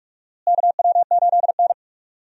Morse code, heard as a keyed tone, about 44 wpm, 700 Hz. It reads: KW8N
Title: USA Call Signs - 4 Characters - Repeated 2x - 45wpm